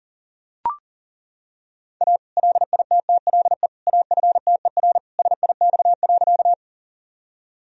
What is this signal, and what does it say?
Morse code, heard as a keyed tone, about 37 wpm, 700 Hz. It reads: A LITTLE AFTER SIX.
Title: Sentences from Top 1000 Words - Review Part 2 - 40wpm